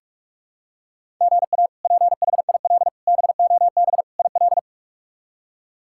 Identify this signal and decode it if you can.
Morse code, heard as a keyed tone, about 46 wpm, 700 Hz. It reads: GA PHIL BOB IL
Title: K1USN SST QSO Practice - 45wpm